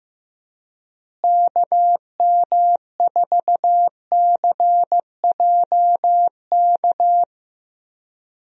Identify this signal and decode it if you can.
Morse code, heard as a keyed tone, about 15 wpm, 700 Hz. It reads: KM4CJK
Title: USA Call Signs - 6 Characters - Repeated 3x - 15wpm